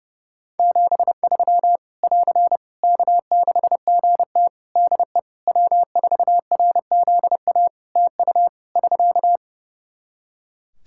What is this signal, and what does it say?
Morse code, heard as a keyed tone, about 30 wpm, 700 Hz. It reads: 73 <AR> K6GT DE W4RZA TU <SK>
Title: Practice QSO - Exchange 4 - Ending - 30wpm